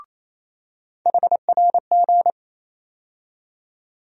Morse code, heard as a keyed tone, about 28 wpm, 700 Hz. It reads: HRG
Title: USA Call Sign Postfixes - 3 Letters 28wpm